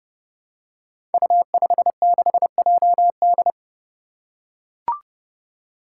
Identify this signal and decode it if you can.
Morse code, heard as a keyed tone, about 30 wpm, 700 Hz. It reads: U56JD
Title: Five Challenging Letter-Number Combinations 30wpm